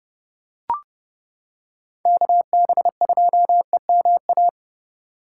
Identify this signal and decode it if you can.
Morse code, heard as a keyed tone, about 29 wpm, 700 Hz. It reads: KB2EMA